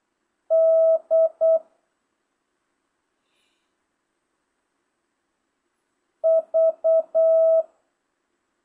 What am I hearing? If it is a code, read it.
Morse code, heard as a keyed tone, about 8 wpm, 660 Hz. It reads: D V
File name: jh8cbh / 2025cw